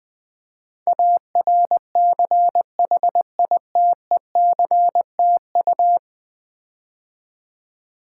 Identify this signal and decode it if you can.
Morse code, heard as a keyed tone, about 20 wpm, 700 Hz. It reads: ARCHITECTU
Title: Build A Word - 5 Letters At A Time - 12-Letter Words - 20wpm